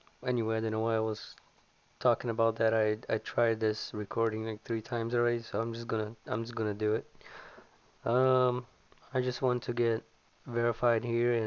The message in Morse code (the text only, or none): none